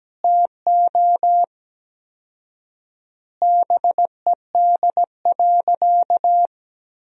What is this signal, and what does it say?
Morse code, heard as a keyed tone, about 17 wpm, 700 Hz. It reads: TO BED.
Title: 4x Word-Spacing - Sentences from Top 500 Words - 17wpm